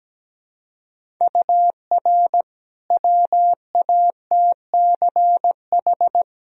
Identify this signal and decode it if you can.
Morse code, heard as a keyed tone, about 17 wpm, 700 Hz. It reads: UR WATCH
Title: Sentences from Top 400 Words and Abbreviations 17wpm